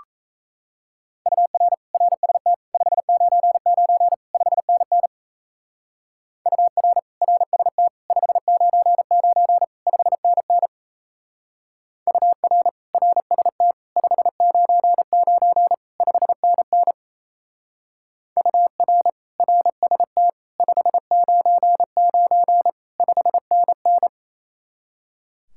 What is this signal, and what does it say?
Morse code, heard as a keyed tone, about 34 wpm, 700 Hz. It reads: UR RST 599 5NN UR RST 599 5NN UR RST 599 5NN UR RST 599 5NN
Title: Speed-Racing Common QSO Phrases - 28wpm